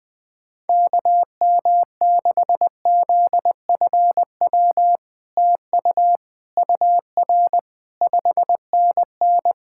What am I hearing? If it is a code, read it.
Morse code, heard as a keyed tone, about 20 wpm, 700 Hz. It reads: KM6ZFW TU UR 5NN
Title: SOTA QSO Practice 20wpm